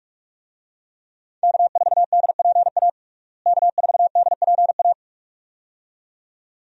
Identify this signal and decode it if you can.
Morse code, heard as a keyed tone, about 46 wpm, 700 Hz. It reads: K4DPA K4DPA
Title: USA Call Signs - 5 Characters - Repeated 2x - 45wpm